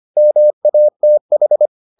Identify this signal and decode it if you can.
Morse code, heard as a keyed tone, about 25 wpm, 600 Hz. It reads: MATH